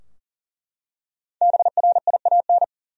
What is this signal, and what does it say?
Morse code, heard as a keyed tone, about 40 wpm, 700 Hz. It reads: BRIAN
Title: USA Common Names - Max 5 Letters - 40wpm